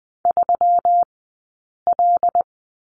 Morse code, heard as a keyed tone, about 20 wpm, spaced as for 8 wpm, 700 Hz. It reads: 3L
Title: 137 Lesson 23d - Letter f - Two Characters 20wpm